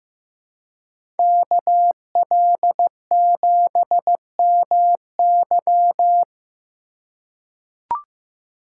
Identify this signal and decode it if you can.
Morse code, heard as a keyed tone, about 15 wpm, 700 Hz. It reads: KL7MY
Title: USA Call Signs - 5 Characters - 3x Character Spacing - 15wpm